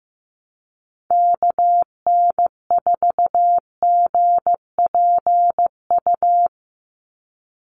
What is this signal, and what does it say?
Morse code, heard as a keyed tone, about 15 wpm, 700 Hz. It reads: KN4GPU
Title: USA Call Signs 15wpm